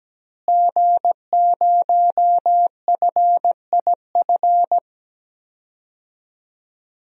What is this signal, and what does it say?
Morse code, heard as a keyed tone, about 17 wpm, 700 Hz. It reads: G0FIF
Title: DXCC-100 Call Signs - 5 Characters - Repeated 3x - 17wpm